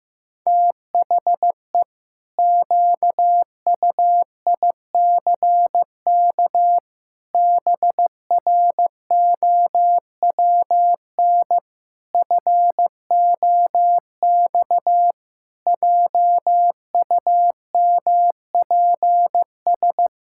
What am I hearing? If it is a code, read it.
Morse code, heard as a keyed tone, about 15 wpm, 700 Hz. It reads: THE QUICK BROWN FOX JUMPS
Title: Pangram Challenge v2 - 15wpm